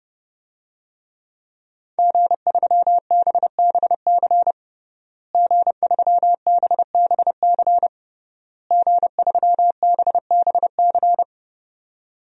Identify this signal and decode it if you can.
Morse code, heard as a keyed tone, about 30 wpm, 700 Hz. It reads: G3BBC G3BBC G3BBC
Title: DXCC-100 Call Signs - 5 Characters - Repeated 3x - 30wpm